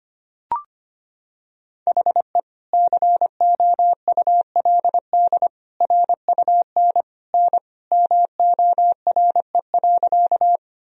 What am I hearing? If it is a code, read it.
Morse code, heard as a keyed tone, about 25 wpm, 700 Hz. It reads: HE COULD RUN N MORE.
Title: Sentences from Top 300 Words and Abbreviations 25wpm